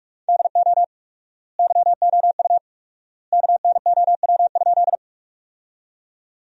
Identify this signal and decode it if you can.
Morse code, heard as a keyed tone, about 46 wpm, 700 Hz. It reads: DO YOU KNOW?
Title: 4x Word-Spacing - Sentences from Top 100 Words - 45wpm